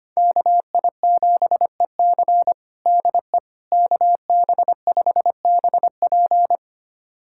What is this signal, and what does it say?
Morse code, heard as a keyed tone, about 25 wpm, 700 Hz. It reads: KI7EC DE KB5BP
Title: USA Callsign DE Callsign - 5 Characters - 25wpm